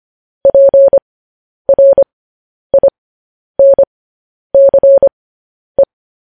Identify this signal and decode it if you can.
Morse code, heard as a keyed tone, about 25 wpm, spaced as for 10 wpm, 550 Hz. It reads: PRINCE
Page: news headlines in morse code at 10 wpm